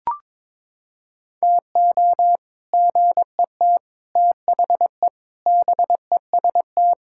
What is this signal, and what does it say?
Morse code, heard as a keyed tone, about 22 wpm, 700 Hz. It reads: TO GET THE BEST
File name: practice / 22wpm / Sentences with 3-Word Phrases - 22wpm